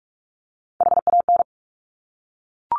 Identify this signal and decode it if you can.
Morse code, heard as a keyed tone, about 46 wpm, 700 Hz. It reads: HAN
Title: Top Three Letter Combinations in Words 45wpm